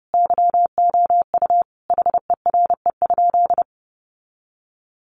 Morse code, heard as a keyed tone, about 30 wpm, 700 Hz. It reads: YOU HERE?
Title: Sentences from Top 500 Words 30wpm